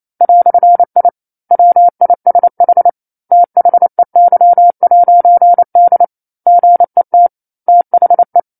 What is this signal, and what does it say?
Morse code, heard as a keyed tone, about 28 wpm, 700 Hz. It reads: "I WISH THEY'D GET THE
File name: 09-Chapter-09 28 WPM